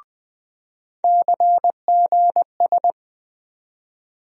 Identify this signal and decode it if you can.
Morse code, heard as a keyed tone, about 21 wpm, 700 Hz. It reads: CGS
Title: USA Call Sign Postfixes - 3 Letters 20wpm